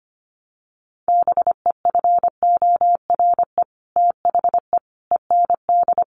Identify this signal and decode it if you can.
Morse code, heard as a keyed tone, about 25 wpm, 700 Hz. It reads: BEFORE THE END
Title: Sets of 5 Words Encore 25wpm